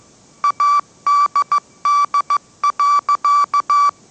Morse code, heard as a keyed tone, about 15 wpm, 1160 Hz. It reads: ADD.